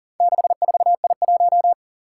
Morse code, heard as a keyed tone, about 37 wpm, 700 Hz. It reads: 64I1
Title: Four Challenging Letter-Number Combinations 40wpm